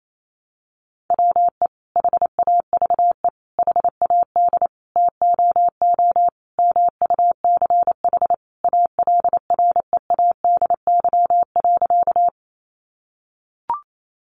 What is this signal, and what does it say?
Morse code, heard as a keyed tone, about 28 wpm, 700 Hz. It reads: WE HAVE HAD TOO MUCH ALREADY.
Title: Sentences from Top 700 Words 28wpm